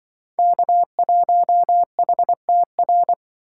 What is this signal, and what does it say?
Morse code, heard as a keyed tone, about 25 wpm, 700 Hz. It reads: K1HTR
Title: Speed Racing - USA Call Signs 20wpm